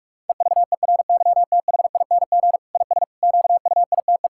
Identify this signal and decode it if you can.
Morse code, heard as a keyed tone, about 46 wpm, 700 Hz. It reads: EVERYTHING IS QUITE
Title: Sentences from Top 1000 Words - Review Part 6 - 45wpm